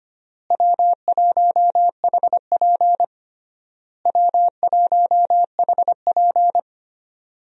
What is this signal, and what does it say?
Morse code, heard as a keyed tone, about 25 wpm, 700 Hz. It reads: W1HP W1HP
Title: USA Call Signs - 4 Characters - Repeated 3x - 25wpm